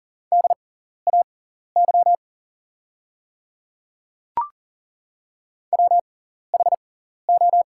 Morse code, heard as a keyed tone, about 40 wpm, spaced as for 14 wpm, 700 Hz. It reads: DAY WHO
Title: 6x Character Spacing - Top 200 Words - 40wpm